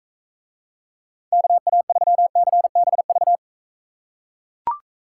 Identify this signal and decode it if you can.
Morse code, heard as a keyed tone, about 41 wpm, 700 Hz. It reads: KA3CBV